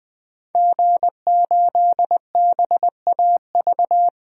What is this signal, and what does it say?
Morse code, heard as a keyed tone, about 20 wpm, 700 Hz. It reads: G8BAV
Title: DXCC-100 Call Signs - 5 Characters - 20wpm